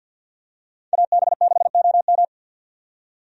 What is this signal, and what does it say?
Morse code, heard as a keyed tone, about 47 wpm, 700 Hz. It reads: AB6OM
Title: USA Call Signs - 5 Characters - 50wpm